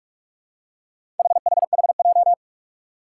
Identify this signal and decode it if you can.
Morse code, heard as a keyed tone, about 44 wpm, 700 Hz. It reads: HHHJ
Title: Four Challenging Letter-Number Combinations 45wpm